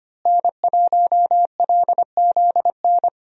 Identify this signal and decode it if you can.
Morse code, heard as a keyed tone, about 25 wpm, 700 Hz. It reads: N1LZN
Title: USA Call Signs - 5 Characters - Repeated 3x - 25wpm